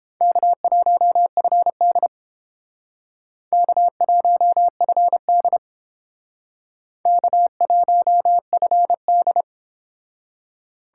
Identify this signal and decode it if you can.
Morse code, heard as a keyed tone, about 30 wpm, 700 Hz. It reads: K1FD K1FD K1FD